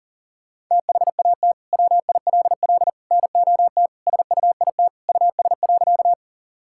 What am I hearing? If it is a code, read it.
Morse code, heard as a keyed tone, about 40 wpm, 700 Hz. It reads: THAT WILL NOT SUIT US.
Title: Sentences from Top 1600 Words 40wpm